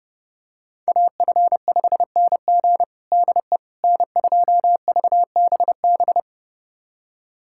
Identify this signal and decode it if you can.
Morse code, heard as a keyed tone, about 30 wpm, 700 Hz. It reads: AF5NG DE N2VBB
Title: USA Callsign DE Callsign - 5 Characters - 30wpm